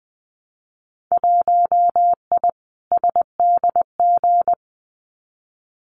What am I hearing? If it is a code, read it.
Morse code, heard as a keyed tone, about 20 wpm, 700 Hz. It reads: 1I SDG
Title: Winter Field Day Exchanges 20wpm